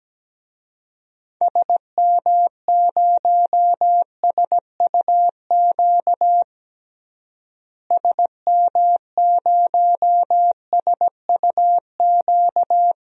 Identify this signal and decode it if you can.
Morse code, heard as a keyed tone, about 17 wpm, 700 Hz. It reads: SM0SUQ SM0SUQ
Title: DXCC-100 Call Signs - 6 Characters - Repeated 3x - 17wpm